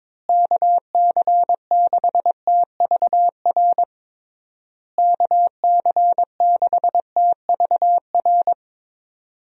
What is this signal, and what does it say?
Morse code, heard as a keyed tone, about 22 wpm, 700 Hz. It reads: KC6TVR KC6TVR
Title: USA Call Signs - 6 Characters - Repeated 3x - 22wpm